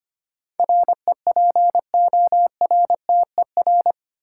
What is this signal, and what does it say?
Morse code, heard as a keyed tone, about 25 wpm, 700 Hz. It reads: REPORTER
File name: Eight Letter Words - 25wpm